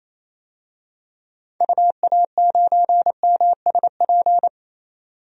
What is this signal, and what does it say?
Morse code, heard as a keyed tone, about 28 wpm, 700 Hz. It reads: UA9MSP
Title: DXCC-100 Call Signs - 6 Characters - 28wpm